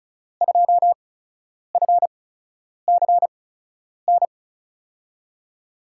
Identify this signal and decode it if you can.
Morse code, heard as a keyed tone, about 36 wpm, spaced as for 10 wpm, 700 Hz. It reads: 2FCN
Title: USA Call Signs - 8x Character Spacing - 35wpm